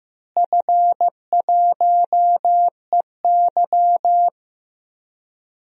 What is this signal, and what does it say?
Morse code, heard as a keyed tone, about 15 wpm, 700 Hz. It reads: F1EY
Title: DXCC-100 Call Signs - 4 Characters - Repeated 3x - 15wpm